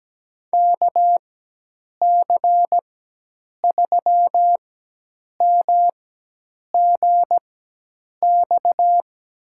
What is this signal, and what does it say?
Morse code, heard as a keyed tone, about 17 wpm, spaced as for 8 wpm, 700 Hz. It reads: KC3MGX
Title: USA Call Signs - 4x Character Spacing - 17wpm